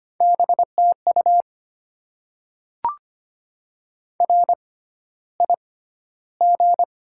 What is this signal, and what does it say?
Morse code, heard as a keyed tone, about 25 wpm, spaced as for 13 wpm, 690 Hz. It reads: BTU RIG
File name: Top 20 QSO Elements - 6x Character Spacing - 25wpm